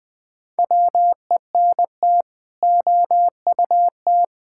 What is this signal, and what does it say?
Morse code, heard as a keyed tone, about 20 wpm, 700 Hz. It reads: WENT OUT
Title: Sets of 6 Words - 20wpm